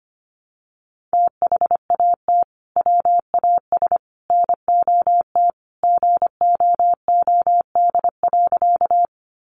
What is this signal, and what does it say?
Morse code, heard as a keyed tone, about 25 wpm, 700 Hz. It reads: THAT WAS NOT GOOD.
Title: Sentences from Top 600 Words 25wpm